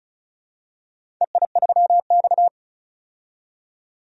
Morse code, heard as a keyed tone, about 34 wpm, 700 Hz. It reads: EI3X
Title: DXCC-100 Call Signs - 4 Characters - 35wpm